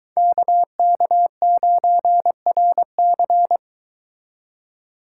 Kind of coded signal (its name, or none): Morse code